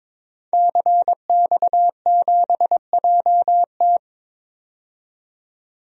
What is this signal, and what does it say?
Morse code, heard as a keyed tone, about 22 wpm, 700 Hz. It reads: CX7JT
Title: DXCC-100 Call Signs - 5 Characters - 22wpm